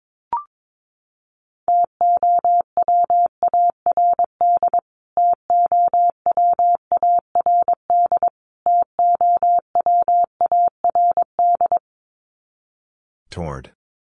Morse code, heard as a keyed tone, about 22 wpm, 700 Hz. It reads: TOWARD TOWARD TOWARD
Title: Top 500 Words Repeated 22wpm